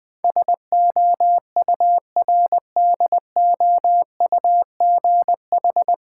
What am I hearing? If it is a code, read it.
Morse code, heard as a keyed tone, about 21 wpm, 690 Hz. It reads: SOURDOUGH
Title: Difficult Words with Hints - 20wpm